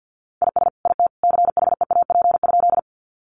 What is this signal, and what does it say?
Morse code, heard as a keyed tone, about 50 wpm, 700 Hz. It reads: IS IT CHEAP?
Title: Sentences from Top 1200 Words 50wpm